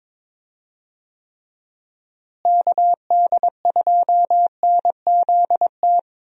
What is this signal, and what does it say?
Morse code, heard as a keyed tone, about 22 wpm, 700 Hz. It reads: KD2NZT
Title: USA Call Signs - 6 Characters - Repeated 2x - 22wpm